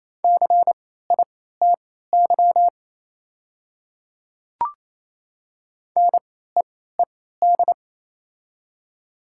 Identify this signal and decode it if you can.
Morse code, heard as a keyed tone, about 27 wpm, spaced as for 16 wpm, 700 Hz. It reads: CITY NEED
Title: 3x Character Spacing - Top 300 Words - 28wpm